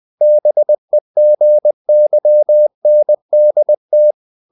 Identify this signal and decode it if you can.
Morse code, heard as a keyed tone, about 20 wpm, 600 Hz. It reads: BEGYNDT
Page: Ferie 20 wpm